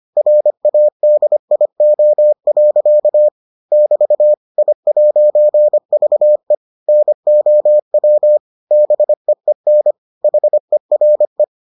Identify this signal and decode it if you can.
Morse code, heard as a keyed tone, about 25 wpm, 600 Hz. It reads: RADIO. <BT> I'VE NOW BEEN HERE